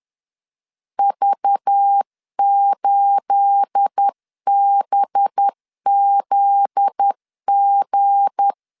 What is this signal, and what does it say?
Morse code, heard as a keyed tone, about 10 wpm, 800 Hz. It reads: V8BZG
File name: LEZIONI / 9 28 10wpm 2842